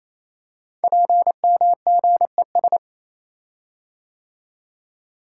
Morse code, heard as a keyed tone, about 29 wpm, 700 Hz. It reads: PMGES
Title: Code Groups - 5 Letters - 28wpm